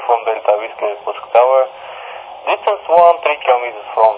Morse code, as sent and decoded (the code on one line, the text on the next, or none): none